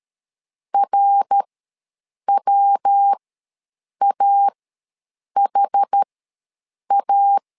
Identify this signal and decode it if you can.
Morse code, heard as a keyed tone, about 13 wpm, spaced as for 7 wpm, 800 Hz. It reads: RWAHA
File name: LEZIONI / 3 MWRO / 3 MWR ws 12wpm 1809